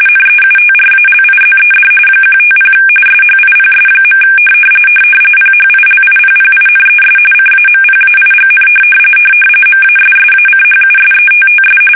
RTTY, about 45.45 baud. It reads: NEUTRALIZED 150 SHIPS. PHILIPPINES CAMPAIGN MASSACHUSETTS
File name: fsk850 / USS-Massachusetts